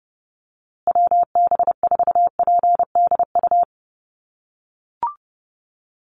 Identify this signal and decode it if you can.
Morse code, heard as a keyed tone, about 30 wpm, 700 Hz. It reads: WB4PDU
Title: USA Call Signs Repeated 2x 30wpm